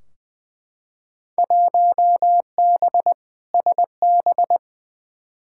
Morse code, heard as a keyed tone, about 20 wpm, 700 Hz. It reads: 1B SB
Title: Field Day Exchanges - 20wpm